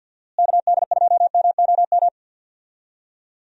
Morse code, heard as a keyed tone, about 50 wpm, 700 Hz. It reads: KD2MOM
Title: USA Call Signs - 6 Characters - Repeated 3x - 50wpm